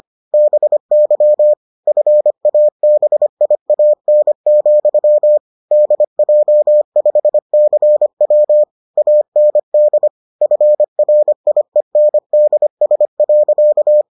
Morse code, heard as a keyed tone, about 25 wpm, 600 Hz. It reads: BY FABIAN, DJ5CW AND FRIENDS.